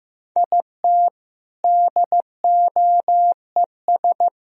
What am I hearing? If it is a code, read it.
Morse code, heard as a keyed tone, about 16 wpm, 700 Hz. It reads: IT DOES
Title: Speed-Racing - Sets of 4 Words 15wpm